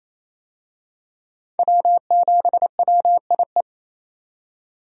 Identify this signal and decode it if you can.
Morse code, heard as a keyed tone, about 28 wpm, 700 Hz. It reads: W7WIE
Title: USA Call Signs - 5 Characters - 28wpm